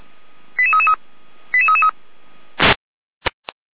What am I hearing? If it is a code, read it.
Five-tone selective call (ZVEI), heard as quick runs of five tones, tones of about 70 ms. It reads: 88333 88333
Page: ZVEI